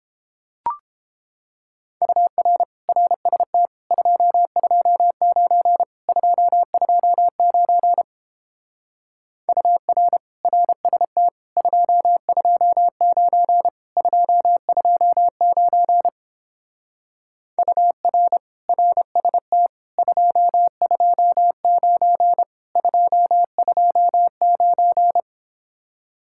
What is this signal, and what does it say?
Morse code, heard as a keyed tone, about 29 wpm, 700 Hz. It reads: UR RST 229 229 UR RST 229 229 UR RST 229 229